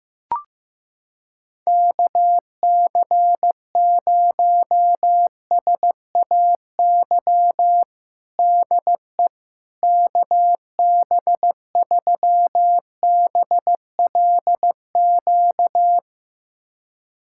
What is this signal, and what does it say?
Morse code, heard as a keyed tone, about 15 wpm, 700 Hz. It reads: KC0SAY DE KB3BLQ
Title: USA Callsign DE Callsign - 6 Characters - 15wpm